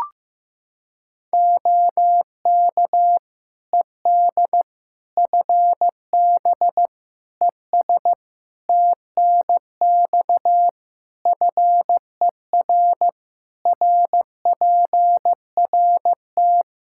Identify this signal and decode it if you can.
Morse code, heard as a keyed tone, about 15 wpm, 700 Hz. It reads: OK ED FB ES TNX FER RPRT
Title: K1USN SST QSO Practice 15wpm